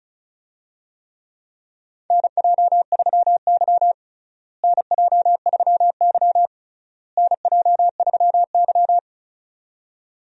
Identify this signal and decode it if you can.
Morse code, heard as a keyed tone, about 36 wpm, 700 Hz. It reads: NJ3Y NJ3Y NJ3Y
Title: USA Call Signs - 4 Characters - Repeated 3x - 35wpm